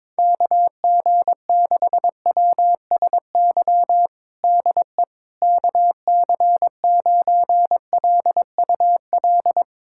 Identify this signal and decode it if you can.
Morse code, heard as a keyed tone, about 22 wpm, 700 Hz. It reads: KG6WSY DE KC9LUL
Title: USA Callsign DE Callsign - 6 Characters - 22wpm